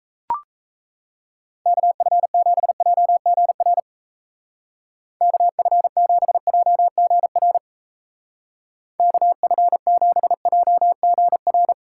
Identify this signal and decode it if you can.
Morse code, heard as a keyed tone, about 35 wpm, 700 Hz. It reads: KF7JGR KF7JGR KF7JGR